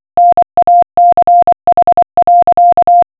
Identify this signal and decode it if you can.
Morse code, heard as a keyed tone, about 24 wpm, 700 Hz. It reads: NACH.